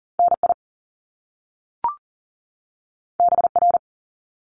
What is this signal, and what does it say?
Morse code, heard as a keyed tone, about 42 wpm, 700 Hz. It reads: NI BR
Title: Top Two Letter Combinations in Words 40wpm